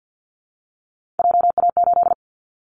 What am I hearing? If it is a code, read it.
Morse code, heard as a keyed tone, about 50 wpm, 700 Hz. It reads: JA8
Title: Callsign Prefixes for Top 25 DXCC - Repeated 3x - 50wpm